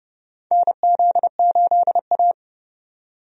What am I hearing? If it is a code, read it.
Morse code, heard as a keyed tone, about 29 wpm, 700 Hz. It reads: NZ8A
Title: USA Call Signs - 4 Characters - 30wpm